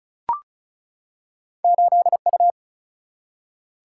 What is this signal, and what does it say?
Morse code, heard as a keyed tone, about 36 wpm, 700 Hz. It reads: 8U
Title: Single-Letter Single-Number 35wpm